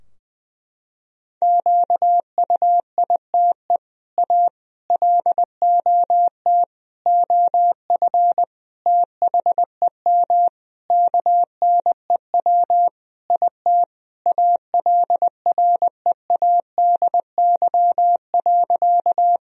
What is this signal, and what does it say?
Morse code, heard as a keyed tone, about 21 wpm, 700 Hz. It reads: QUITE A LOT OF THEM KNEW IT ALREADY.
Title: Sentences from Top 1000 Words - Review Part 3 - 20wpm